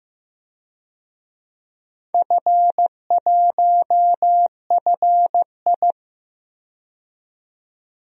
Morse code, heard as a keyed tone, about 16 wpm, 700 Hz. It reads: F1FI
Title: DXCC-100 Call Signs - 4 Characters - 15wpm